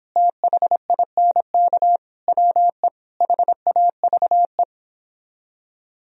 Morse code, heard as a keyed tone, about 29 wpm, 700 Hz. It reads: THINK WE HAVE